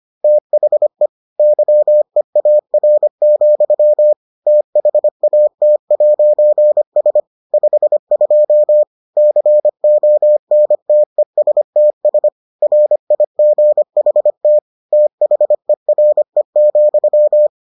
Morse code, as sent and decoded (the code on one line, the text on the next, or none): - .... / -.-- . .- .-. --..-- / - .... .- - .----. ... / ..... ..--- / -.-. --- -. - . ... - ... / .-. .. --. .... - / - .... .-. --..--
THE YEAR, THAT'S 52 CONTESTS RIGHT THERE,